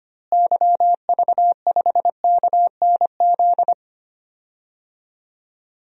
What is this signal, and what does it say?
Morse code, heard as a keyed tone, about 25 wpm, 700 Hz. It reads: YV5KNZ
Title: DXCC-100 Call Signs - 6 Characters - Repeated 3x - 25wpm